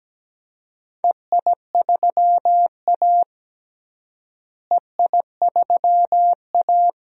Morse code, heard as a keyed tone, about 17 wpm, 700 Hz. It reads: EI3A EI3A
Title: DXCC-100 Call Signs - 4 Characters - Repeated 2x - 17wpm